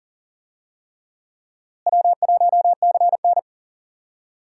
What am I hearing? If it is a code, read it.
Morse code, heard as a keyed tone, about 37 wpm, 700 Hz. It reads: W1CN
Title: USA Call Signs - 4 Characters - Repeated 3x - 40wpm